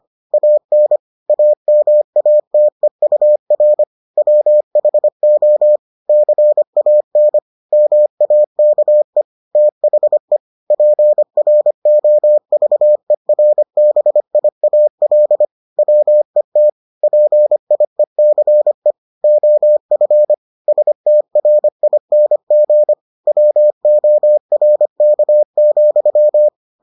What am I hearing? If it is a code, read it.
Morse code, heard as a keyed tone, about 25 wpm, 600 Hz. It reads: AN AMATEUR WHO CAN MAKE THE PROVERBIAL WET PIECE OF STRING WORK,